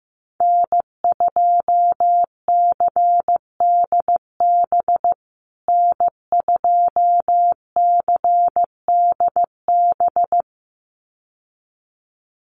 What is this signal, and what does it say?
Morse code, heard as a keyed tone, about 14 wpm, 700 Hz. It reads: N2CDB N2CDB
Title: USA Call Signs Repeated 2x 15wpm